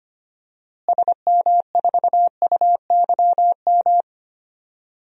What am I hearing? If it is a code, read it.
Morse code, heard as a keyed tone, about 25 wpm, 700 Hz. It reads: SM4UYM
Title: DXCC-100 Call Signs - 6 Characters - Repeated 2x - 25wpm